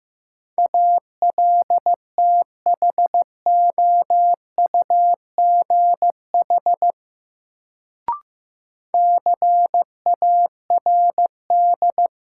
Morse code, heard as a keyed tone, about 15 wpm, 700 Hz. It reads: ALTHOUGH CARD
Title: Sentences from Top 600 Words - New Word Drill - 15wpm